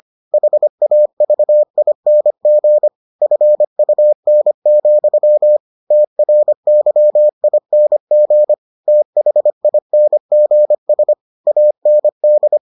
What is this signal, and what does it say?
Morse code, heard as a keyed tone, about 25 wpm, 600 Hz. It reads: HAVING FUN, TRYING THINGS AND